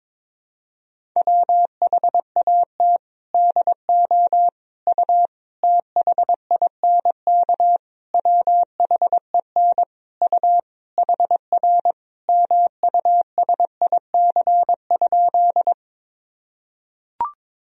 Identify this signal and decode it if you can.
Morse code, heard as a keyed tone, about 22 wpm, 700 Hz. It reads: WHAT DO U THINK WHEN U HR MUSIC?
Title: Sentences from Top 300 Words and Abbreviations 22wpm